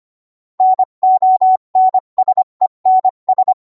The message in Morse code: NONSENS